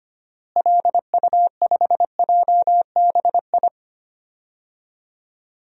Morse code, heard as a keyed tone, about 25 wpm, 700 Hz. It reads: LU5JBI
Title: DXCC-100 Call Signs - 6 Characters - Repeated 3x - 25wpm